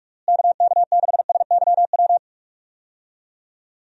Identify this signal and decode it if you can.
Morse code, heard as a keyed tone, about 46 wpm, 700 Hz. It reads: KK6SYW
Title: USA Call Signs - 6 Characters - Repeated 3x - 45wpm